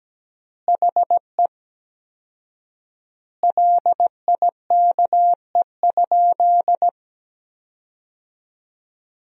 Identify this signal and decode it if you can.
Morse code, heard as a keyed tone, about 17 wpm, 700 Hz. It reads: HE LIKE?
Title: 4x Word-Spacing - Sentences from Top 100 Words - 17wpm